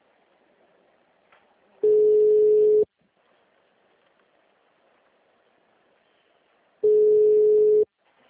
Ringback tone (Europe, 425 Hz).